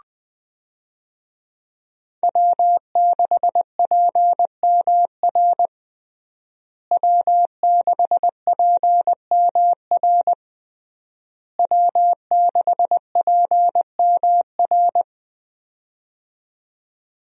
Morse code, heard as a keyed tone, about 21 wpm, 700 Hz. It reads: W6PMR W6PMR W6PMR
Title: USA Call Signs - 5 Characters - Repeated 3x - 20wpm